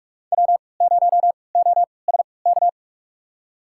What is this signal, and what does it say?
Morse code, heard as a keyed tone, about 43 wpm, spaced as for 25 wpm, 700 Hz. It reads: W0OSK
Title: USA Call Signs - 5 Characters - 3x Character Spacing - 45wpm